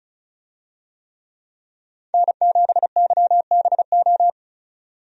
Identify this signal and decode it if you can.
Morse code, heard as a keyed tone, about 36 wpm, 700 Hz. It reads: N7YBO
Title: USA Call Signs - 5 Characters - 35wpm